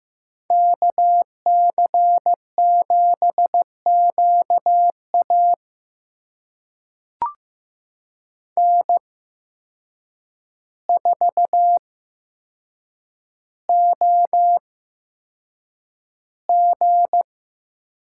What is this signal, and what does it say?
Morse code, heard as a keyed tone, about 15 wpm, spaced as for 7 wpm, 700 Hz. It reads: KC7QA N4OG
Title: USA Call Signs - 5 Characters - 8x Character Spacing - 15wpm